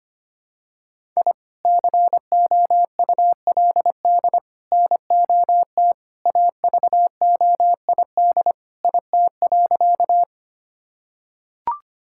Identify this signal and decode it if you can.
Morse code, heard as a keyed tone, about 25 wpm, 700 Hz. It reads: I COULD NOT AVOID IT.
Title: Sentences from Top 1000 Words - Review Part 6 - 25wpm